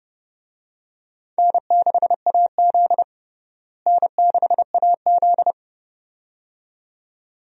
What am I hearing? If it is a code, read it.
Morse code, heard as a keyed tone, about 29 wpm, 700 Hz. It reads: N6AZ N6AZ
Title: USA Call Signs - 4 Characters - Repeated 2x - 30wpm